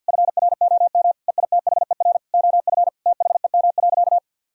Morse code, heard as a keyed tone, about 50 wpm, 700 Hz. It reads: FROM EITHER OF THEM.
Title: Sentences from Top 1000 Words - Review Part 4 - 50wpm